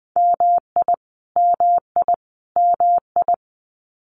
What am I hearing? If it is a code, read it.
Morse code, heard as a keyed tone, about 20 wpm, 700 Hz. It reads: MI MI MI
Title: US States Repeated 3x 20wpm